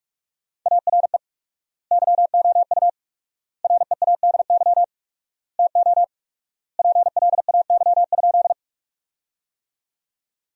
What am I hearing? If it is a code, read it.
Morse code, heard as a keyed tone, about 45 wpm, 700 Hz. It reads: ARE YOU READY TO PLAY?
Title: 4x Word-Spacing - Sentences from Top 400 Words - 45wpm